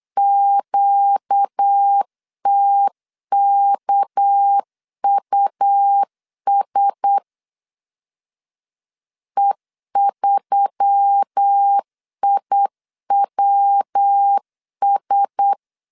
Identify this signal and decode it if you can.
Morse code, heard as a keyed tone, about 8 wpm, 800 Hz. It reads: QTKUS E3IWS
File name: LEZIONI / 6 PKX0 / 6 PK 8wpm 3003